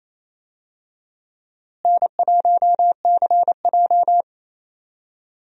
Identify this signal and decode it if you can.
Morse code, heard as a keyed tone, about 28 wpm, 700 Hz. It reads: N1CJ
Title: USA Call Signs - 4 Characters - Repeated 2x - 28wpm